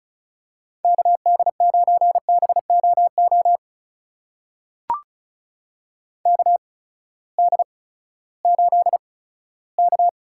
Morse code, heard as a keyed tone, about 36 wpm, spaced as for 18 wpm, 700 Hz. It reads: KD9BOO KD8K
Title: USA Call Signs - 6 Characters - 8x Character Spacing - 35wpm